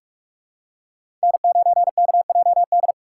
Morse code, heard as a keyed tone, about 46 wpm, 700 Hz. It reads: N9KJD
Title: USA Call Signs - 5 Characters - 4x Character Spacing - 45wpm